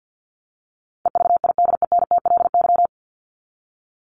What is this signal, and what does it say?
Morse code, heard as a keyed tone, about 50 wpm, 700 Hz. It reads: EVIDENTLY
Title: LY Suffix - Top 100 Words with Base Words 50wpm